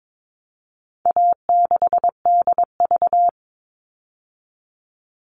Morse code, heard as a keyed tone, about 22 wpm, 700 Hz. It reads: A6DV
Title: All Cut Numbers - 4-Digit 22wpm